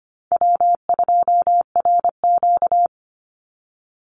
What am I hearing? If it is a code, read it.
Morse code, heard as a keyed tone, about 25 wpm, 700 Hz. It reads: W2RQ
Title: Easy Call Signs Repeated 3x - 25wpm